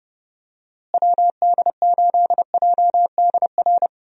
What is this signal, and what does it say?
Morse code, heard as a keyed tone, about 30 wpm, 700 Hz. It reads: WD8JDR
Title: USA Call Signs - 6 Characters - Repeated 3x - 30wpm